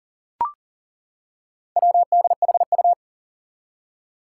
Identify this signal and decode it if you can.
Morse code, heard as a keyed tone, about 40 wpm, 700 Hz. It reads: WDHU
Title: Four Challenging Letter-Number Combinations 40wpm